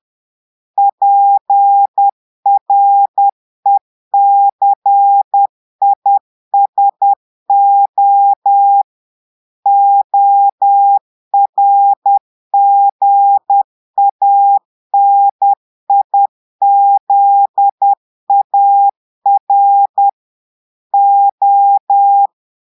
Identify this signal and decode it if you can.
Morse code, heard as a keyed tone, about 10 wpm, 800 Hz. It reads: PRECISO ORGANIZAR O
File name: portugais texte en clair 2 vitesse 10 mots minutes